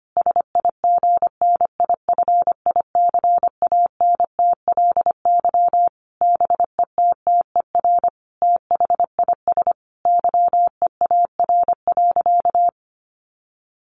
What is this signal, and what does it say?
Morse code, heard as a keyed tone, about 25 wpm, 700 Hz. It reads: SIGNIFICANTLY BETTER THIS YEAR.
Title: Sentences from Top 1500 Words 25wpm